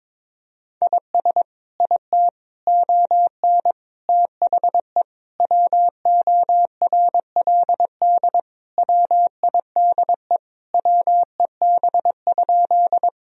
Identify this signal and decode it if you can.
Morse code, heard as a keyed tone, about 22 wpm, 700 Hz. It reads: IS IT ON THE WORLD WIDE WEB?
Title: Sentences from Top 1000 Words - Review Part 4 - 22wpm